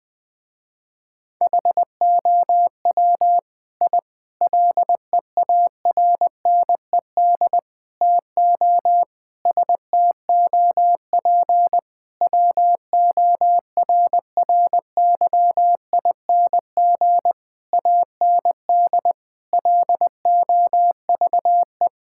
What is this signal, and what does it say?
Morse code, heard as a keyed tone, about 19 wpm, 700 Hz. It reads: HOW I LEARNED TO STOP WORRYING AND LOVE
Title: Sentences with 8-Word Phrases - Spoken Beginning - 20wpm